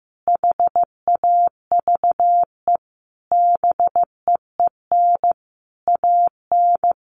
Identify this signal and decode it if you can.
Morse code, heard as a keyed tone, about 15 wpm, 700 Hz. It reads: HAVE BEEN AN
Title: Sentences from Top 1100 Words 15wpm